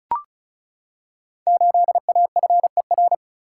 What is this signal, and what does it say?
Morse code, heard as a keyed tone, about 34 wpm, 690 Hz. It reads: 8AFER
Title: Code Groups - 5 Letter:Numbers - 35wpm